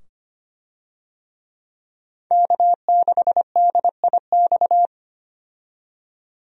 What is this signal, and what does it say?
Morse code, heard as a keyed tone, about 25 wpm, 700 Hz. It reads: K6DIX
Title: USA Call Signs - 5 Characters - 25wpm